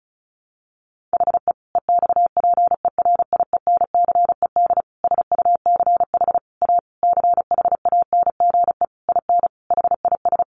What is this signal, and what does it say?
Morse code, heard as a keyed tone, about 34 wpm, 700 Hz. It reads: HE EXPERIENCED SUCH A CHANGE IN HIS
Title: Sentences from Top 1100 Words 35wpm